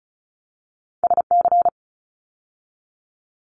Morse code, heard as a keyed tone, about 36 wpm, 700 Hz. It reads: SC
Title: ARRL Sections 35wpm